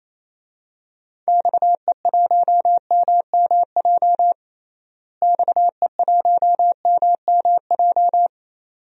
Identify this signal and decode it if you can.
Morse code, heard as a keyed tone, about 27 wpm, 700 Hz. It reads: XE1MMJ XE1MMJ
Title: DXCC-100 Call Signs - 6 Characters - Repeated 2x - 28wpm